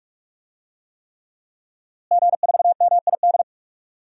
Morse code, heard as a keyed tone, about 42 wpm, 700 Hz. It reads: G4MID
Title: DXCC-100 Call Signs - 5 Characters - Repeated 3x - 45wpm